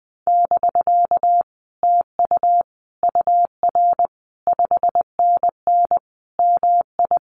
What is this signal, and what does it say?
Morse code, heard as a keyed tone, about 19 wpm, 700 Hz. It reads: <BK> TU UR 5NN MI
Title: Common QSO Phrases Encore 20wpm